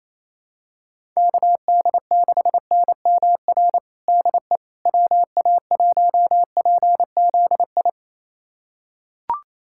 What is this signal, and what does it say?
Morse code, heard as a keyed tone, about 28 wpm, 700 Hz. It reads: KD6NMR DE WA1PZI
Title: USA Callsign DE Callsign 28wpm